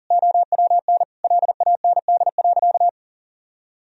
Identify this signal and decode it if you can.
Morse code, heard as a keyed tone, about 40 wpm, 700 Hz. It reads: OWN LAND.
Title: Sentences from Top 1000 Words - Review Part 2 - 40wpm